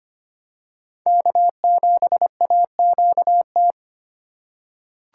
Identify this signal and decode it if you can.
Morse code, heard as a keyed tone, about 25 wpm, 700 Hz. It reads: K7AQT
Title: USA Call Signs - 5 Characters - 3x Character Spacing - 25wpm